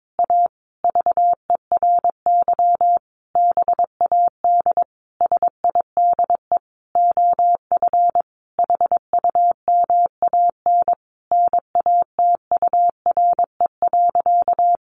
Morse code, heard as a keyed tone, about 22 wpm, 700 Hz. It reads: A VERY BAD SIDE OF HUMAN NATURE.
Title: Sentences from Top 700 Words 22wpm